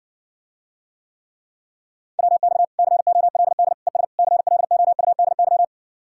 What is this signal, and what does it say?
Morse code, heard as a keyed tone, about 60 wpm, 690 Hz. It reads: WX COLD ES CLOUDY